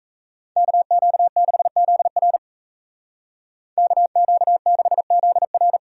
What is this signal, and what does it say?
Morse code, heard as a keyed tone, about 34 wpm, 700 Hz. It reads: KQ6ZR KQ6ZR
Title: Speed Racing - USA Call Signs 28wpm